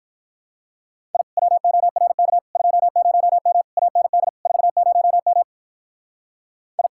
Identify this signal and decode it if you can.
Morse code, heard as a keyed tone, about 44 wpm, 700 Hz. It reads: I WORK 20M AND 40M I
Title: Speed-Racing Common QSO Phrases - 35wpm